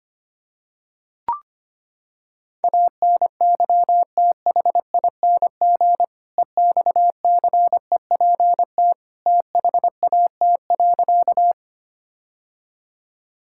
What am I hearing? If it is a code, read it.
Morse code, heard as a keyed tone, about 25 wpm, 700 Hz. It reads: ANYTHING EXCEPT THAT.
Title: Sentences from Top 1600 Words 25wpm